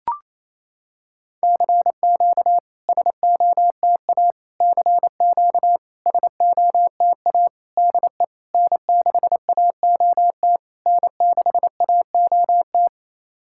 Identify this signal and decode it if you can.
Morse code, heard as a keyed tone, about 28 wpm, 690 Hz. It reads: CQ SOTA CQ SOTA DE N6AOT N6AOT
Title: SOTA QSO Practice 28wpm